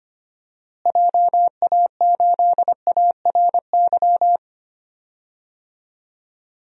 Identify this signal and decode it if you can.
Morse code, heard as a keyed tone, about 25 wpm, 700 Hz. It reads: JA8ARY
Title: DXCC-100 Call Signs - 6 Characters - Repeated 2x - 25wpm